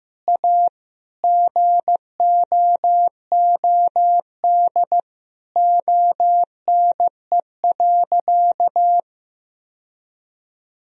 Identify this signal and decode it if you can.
Morse code, heard as a keyed tone, about 16 wpm, 700 Hz. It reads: A GOOD ONE.
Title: Sentences from Top 1000 Words - Review Part 3 - 15wpm